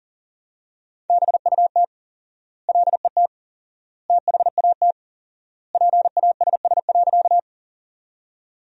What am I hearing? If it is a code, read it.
Morse code, heard as a keyed tone, about 40 wpm, 700 Hz. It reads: BUT LET THAT PASS.
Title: 4x Word-Spacing - Sentences from Top 400 Words - 40wpm